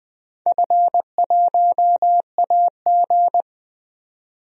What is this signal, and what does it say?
Morse code, heard as a keyed tone, about 20 wpm, 700 Hz. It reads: F1AG
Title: DXCC-100 Call Signs - 4 Characters - 20wpm